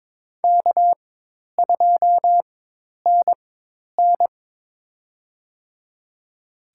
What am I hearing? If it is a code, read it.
Morse code, heard as a keyed tone, about 22 wpm, spaced as for 10 wpm, 700 Hz. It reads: K2NN
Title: USA Call Signs - 4 Characters - 4x Character Spacing - 22wpm